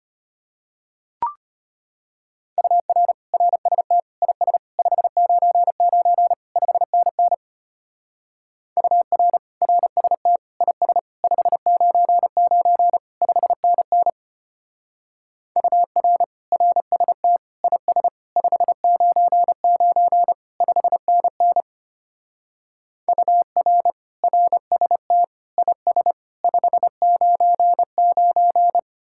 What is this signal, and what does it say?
Morse code, heard as a keyed tone, about 31 wpm, 700 Hz. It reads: UR RST IS 599 5NN UR RST IS 599 5NN UR RST IS 599 5NN UR RST IS 599
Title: Speed-Racing Common QSO Phrases Encore - 25wpm